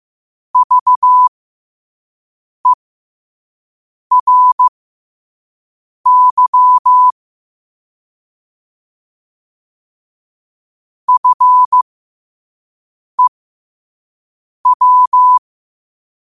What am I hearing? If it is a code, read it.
Morse code, heard as a keyed tone, about 15 wpm, spaced as for 5 wpm, 1000 Hz. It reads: VERY FEW
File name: qotdfiles / 5wpm